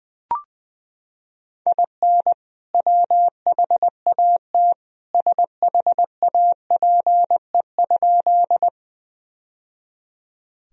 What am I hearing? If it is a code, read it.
Morse code, heard as a keyed tone, about 20 wpm, 700 Hz. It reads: IN WHAT SHAPE?
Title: Sentences from Top 1000 Words - Review Part 3 - 20wpm